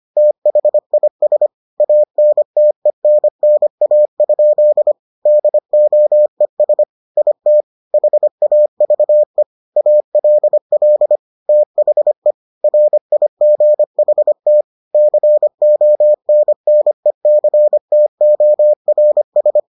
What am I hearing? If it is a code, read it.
Morse code, heard as a keyed tone, about 25 wpm, 600 Hz. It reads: THIS ANTENNA? DOES IT HAVE ALL THE RIGHT CONNECTORS